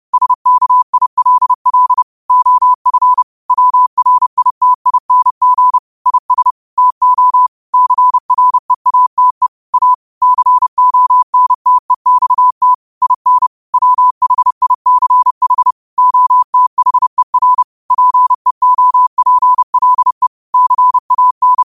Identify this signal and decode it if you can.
Morse code, heard as a keyed tone, about 30 wpm, 1000 Hz. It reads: SKILL OF WRITING IS TO CREATE A CONTEXT IN WHICH OTHER PEOPLE CAN
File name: qotdfiles / 30wpm